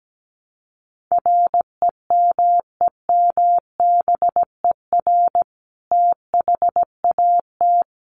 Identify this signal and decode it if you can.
Morse code, heard as a keyed tone, about 17 wpm, 700 Hz. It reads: REMEMBER THAT
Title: Sentences from Top 400 Words 17wpm